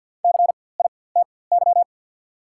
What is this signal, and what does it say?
Morse code, heard as a keyed tone, about 48 wpm, spaced as for 23 wpm, 690 Hz. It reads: CITY
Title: 4x Character Spacing - Top 300 Words - 50wpm